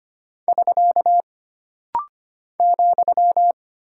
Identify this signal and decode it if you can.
Morse code, heard as a keyed tone, about 25 wpm, 700 Hz. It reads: <SK> ,
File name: Single Punctuation-Prosign - Rapid-Fire - 25wpm